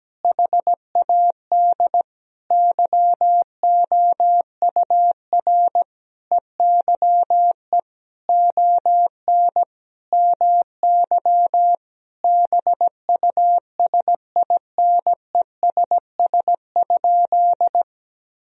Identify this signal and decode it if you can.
Morse code, heard as a keyed tone, about 17 wpm, 700 Hz. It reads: HAD YOUR EYE ON MY BUSINESS?
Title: Sentences from Top 1000 Words - Review Part 1 - 17wpm